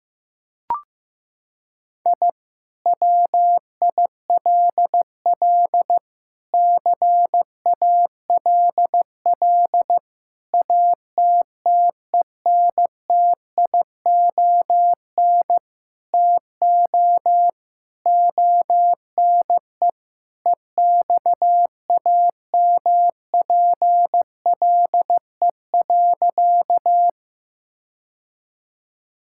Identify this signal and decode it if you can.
Morse code, heard as a keyed tone, about 15 wpm, 700 Hz. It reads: I WILL CALL ATTENTION TO ONE EXAMPLE.
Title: Sentences from Top 1000 Words - Review Part 6 - 15wpm